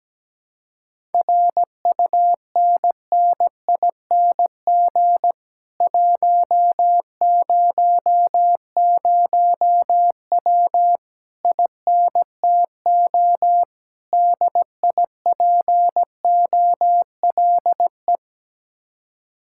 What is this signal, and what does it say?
Morse code, heard as a keyed tone, about 17 wpm, 700 Hz. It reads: RUNNING 100W INTO DIPOLE
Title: Speed-Racing Common QSO Phrases - 17wpm